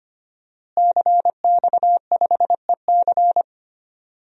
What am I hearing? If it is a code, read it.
Morse code, heard as a keyed tone, about 25 wpm, 700 Hz. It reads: CX5EC
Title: DXCC-100 Call Signs - 5 Characters - Repeated 3x - 25wpm